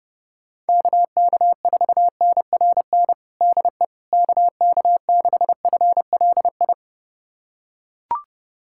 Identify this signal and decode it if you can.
Morse code, heard as a keyed tone, about 29 wpm, 700 Hz. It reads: KK4NRN DE KK6FLI
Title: USA Callsign DE Callsign - 6 Characters - 30wpm